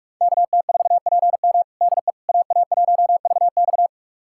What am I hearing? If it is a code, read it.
Morse code, heard as a keyed tone, about 42 wpm, 700 Hz. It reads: KT4PM DE AA1VX
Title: USA Callsign DE Callsign - 5 Characters - 45wpm